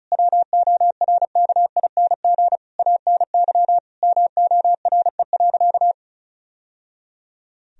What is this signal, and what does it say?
Morse code, heard as a keyed tone, about 34 wpm, 700 Hz. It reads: WORKING ANY MORE.
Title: Sentences from Top 1000 Words - Review Part 4 - 35wpm